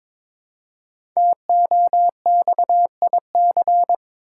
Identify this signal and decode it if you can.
Morse code, heard as a keyed tone, about 22 wpm, 700 Hz. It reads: TOXIC
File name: 5 Letter Words - First 3 Letters Given - 22wpm